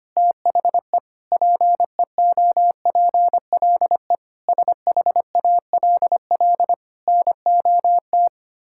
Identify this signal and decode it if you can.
Morse code, heard as a keyed tone, about 25 wpm, 700 Hz. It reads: THE PEOPLE SHALL NOT
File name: practice / 25wpm / Sentences with 9-Word Phrases - 25wpm